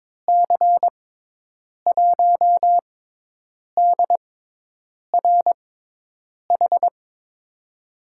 Morse code, heard as a keyed tone, about 22 wpm, spaced as for 8 wpm, 700 Hz. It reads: C1DRH
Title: USA Call Signs - 6 Characters - 6x Character Spacing - 22wpm